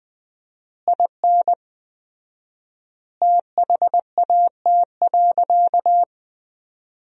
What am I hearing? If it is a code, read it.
Morse code, heard as a keyed tone, about 21 wpm, 700 Hz. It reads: IN THAT.
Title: 4x Word-Spacing - Sentences from Top 200 Words - 20wpm